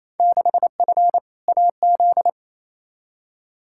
Morse code, heard as a keyed tone, about 28 wpm, 700 Hz. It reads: 6F AZ
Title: Field Day Exchanges Encore - 28wpm